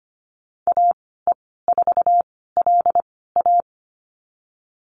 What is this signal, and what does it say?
Morse code, heard as a keyed tone, about 25 wpm, spaced as for 16 wpm, 700 Hz. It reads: AE4LA
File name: Lesson 15g - Number 5 - Call Signs 25wpm